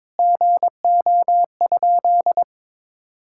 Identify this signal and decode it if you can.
Morse code, heard as a keyed tone, about 22 wpm, 700 Hz. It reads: GO?
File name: practice / 22wpm / Sentences with 2-Word Phrases - 22wpm